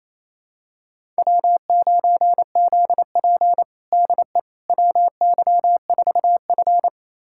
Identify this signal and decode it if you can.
Morse code, heard as a keyed tone, about 27 wpm, 700 Hz. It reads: W9ZP DE WY4F
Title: USA Callsign DE Callsign - 4 Characters - 28wpm